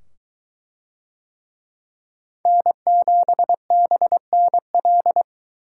Morse code, heard as a keyed tone, about 21 wpm, 700 Hz. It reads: N7BNL